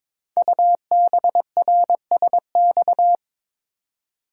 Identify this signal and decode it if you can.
Morse code, heard as a keyed tone, about 22 wpm, 700 Hz. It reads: UBRSX
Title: Code Groups - 5 Letters - 22wpm